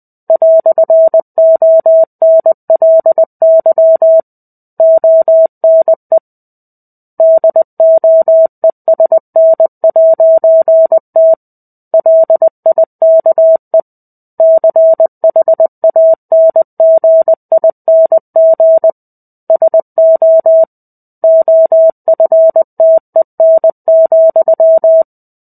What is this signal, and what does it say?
Morse code, heard as a keyed tone, about 20 wpm, 650 Hz. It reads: "ONLY ONE DOESN'T LIKE CHANGING SO OFTEN,
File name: 05-Chapter-05 20 WPM